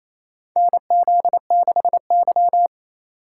Morse code, heard as a keyed tone, about 27 wpm, 700 Hz. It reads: NZ6Y
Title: USA Call Signs - 4 Characters - 2x Character Spacing - 28wpm